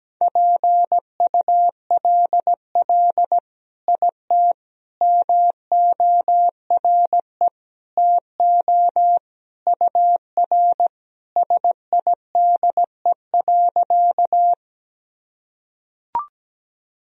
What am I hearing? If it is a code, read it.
Morse code, heard as a keyed tone, about 17 wpm, 700 Hz. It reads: PULL IT MORE TO UR SIDE.
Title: Sentences from Top 400 Words and Abbreviations 17wpm